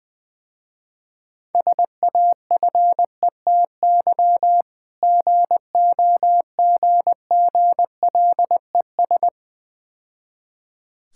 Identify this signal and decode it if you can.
Morse code, heard as a keyed tone, about 21 wpm, 700 Hz. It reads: SAFETY GOGGLES
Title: Difficult Words - Speed-Racing with Hints - 20wpm